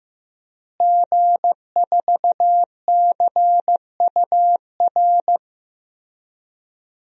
Morse code, heard as a keyed tone, about 15 wpm, 700 Hz. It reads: G4CUR
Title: DXCC-100 Call Signs - 5 Characters - Repeated 2x - 15wpm